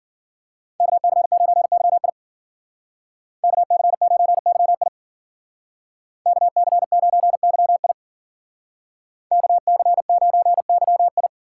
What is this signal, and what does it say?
Morse code, heard as a keyed tone, about 52 wpm, 700 Hz. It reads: KC9YI KC9YI KC9YI KC9YI